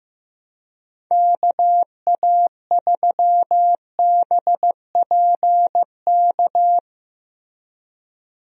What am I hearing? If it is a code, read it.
Morse code, heard as a keyed tone, about 15 wpm, 700 Hz. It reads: KA3BPK
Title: USA Call Signs - 6 Characters - Repeated 3x - 15wpm